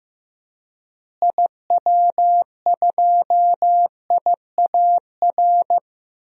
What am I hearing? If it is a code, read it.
Morse code, heard as a keyed tone, about 15 wpm, 700 Hz. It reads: IW2IAR
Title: DXCC-100 Call Signs - 6 Characters - Repeated 2x - 15wpm